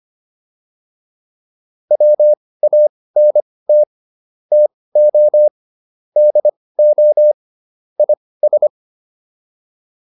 Morse code, heard as a keyed tone, about 25 wpm, spaced as for 18 wpm, 600 Hz. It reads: WANT TO DO IS
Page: intermediate-practice-files